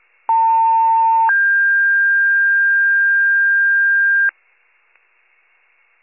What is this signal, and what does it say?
Two-tone page, heard as a one-second tone, then a three-second tone, about 900 then 1600 Hz, with noise. It is heard from 0.3 to 4.3 s.